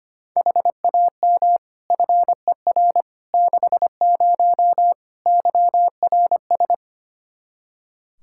Morse code, heard as a keyed tone, about 25 wpm, 700 Hz. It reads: HAM FER 60 YRS
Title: Practice QSO - Exchange 3 - More Info - 25wpm